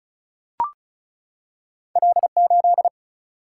Morse code, heard as a keyed tone, about 36 wpm, 700 Hz. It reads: L8
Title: Single-Letter Single-Number 35wpm